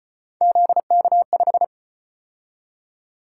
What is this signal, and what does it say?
Morse code, heard as a keyed tone, about 33 wpm, 700 Hz. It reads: ZK5